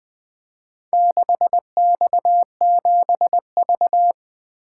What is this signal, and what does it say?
Morse code, heard as a keyed tone, about 20 wpm, 700 Hz. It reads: 6X7V
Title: Four Letter-Number Combinations - 20wpm